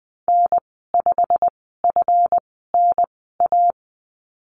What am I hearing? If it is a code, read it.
Morse code, heard as a keyed tone, about 20 wpm, spaced as for 15 wpm, 700 Hz. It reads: N5FNA
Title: 140 Lesson 23g - Letter f - Call Signs 20wpm